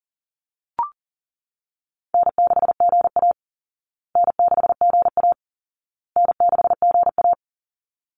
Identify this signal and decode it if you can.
Morse code, heard as a keyed tone, about 40 wpm, 700 Hz. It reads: N6GA N6GA N6GA
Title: Easy Call Signs Repeated 3x - 40wpm